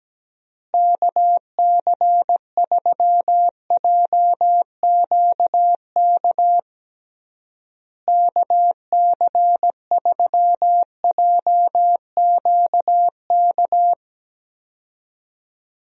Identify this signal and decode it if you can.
Morse code, heard as a keyed tone, about 17 wpm, 700 Hz. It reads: KC3JQK KC3JQK
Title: USA Call Signs - 6 Characters - Repeated 3x - 17wpm